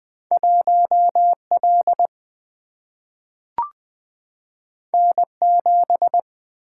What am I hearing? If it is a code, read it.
Morse code, heard as a keyed tone, about 20 wpm, 700 Hz. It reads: 1 L N 7
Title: Single-Letter Single-Number 20wpm